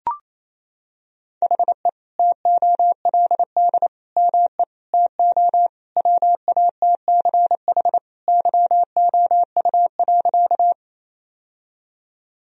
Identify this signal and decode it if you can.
Morse code, heard as a keyed tone, about 28 wpm, 700 Hz. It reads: HE TOLD ME TO WATCH YOU.
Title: Sentences from Top 1000 Words - Review Part 1 - 28wpm